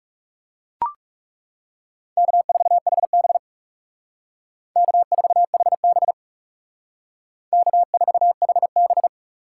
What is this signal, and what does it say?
Morse code, heard as a keyed tone, about 36 wpm, 700 Hz. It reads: K4HB K4HB K4HB
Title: Speed Racing - USA Call Signs 30wpm